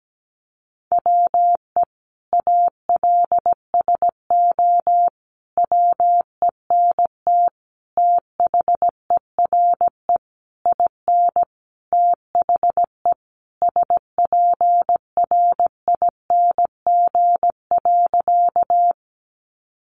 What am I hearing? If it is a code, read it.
Morse code, heard as a keyed tone, about 17 wpm, 700 Hz. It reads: WE ALSO WENT THERE IN THE SPRING.
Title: Sentences from Top 1200 Words 17wpm